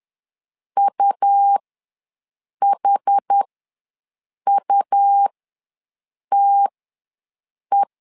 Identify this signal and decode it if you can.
Morse code, heard as a keyed tone, about 10 wpm, spaced as for 6 wpm, 800 Hz. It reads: UHUTE
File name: LEZIONI / 2 UHDN / 2 UH ws 10wpm 2029